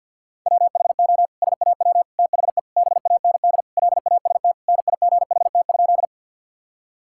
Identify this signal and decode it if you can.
Morse code, heard as a keyed tone, about 50 wpm, 700 Hz. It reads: WHO SAW THE BAND LAST NIGHT?
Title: Sentences from Top 1600 Words 50wpm